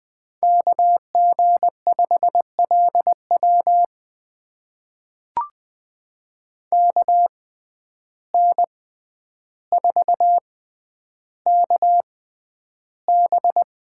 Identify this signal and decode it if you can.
Morse code, heard as a keyed tone, about 20 wpm, spaced as for 10 wpm, 700 Hz. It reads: KG5LW KN4KB
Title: USA Call Signs - 6x Character Spacing - 20wpm